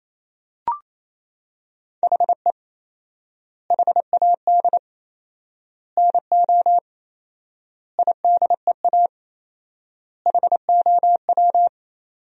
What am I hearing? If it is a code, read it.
Morse code, heard as a keyed tone, about 28 wpm, 700 Hz. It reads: HE HAD NO IDEA HOW
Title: 4x Word-Spacing - Sentences from Top 300 Words - 28wpm